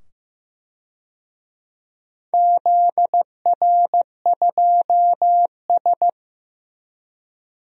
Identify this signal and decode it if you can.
Morse code, heard as a keyed tone, about 15 wpm, 700 Hz. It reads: ZR2S
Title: DXCC-100 Call Signs - 4 Characters - 15wpm